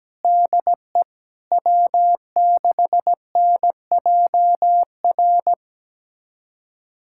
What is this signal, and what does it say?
Morse code, heard as a keyed tone, about 17 wpm, 700 Hz. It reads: DE W6NJR
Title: USA Callsign DE Callsign - 5 Characters - 17wpm